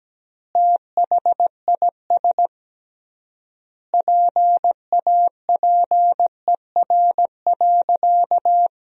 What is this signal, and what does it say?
Morse code, heard as a keyed tone, about 17 wpm, 700 Hz. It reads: THIS PAPER.
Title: 3x Word-Spacing - Sentences from Top 500 Words - 17wpm